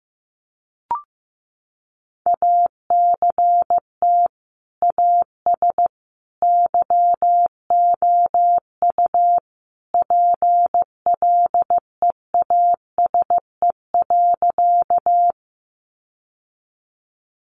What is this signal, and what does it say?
Morse code, heard as a keyed tone, about 15 wpm, 700 Hz. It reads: ACT AS YOU PLEASE.
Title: Sentences from Top 1500 Words 15wpm